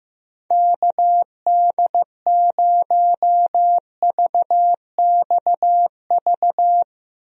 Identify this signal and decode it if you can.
Morse code, heard as a keyed tone, about 15 wpm, 700 Hz. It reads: KD0VXV
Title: USA Call Signs - 6 Characters - Repeated 3x - 15wpm